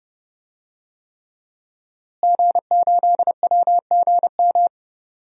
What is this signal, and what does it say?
Morse code, heard as a keyed tone, about 32 wpm, 700 Hz. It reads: G8WGM
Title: DXCC-100 Call Signs - 5 Characters - Repeated 3x - 30wpm